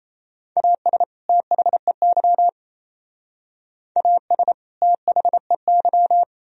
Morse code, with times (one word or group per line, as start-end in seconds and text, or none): 0.6-1.0 AS
1.3-2.5 THEY
4.0-4.5 AS
4.8-6.2 THEY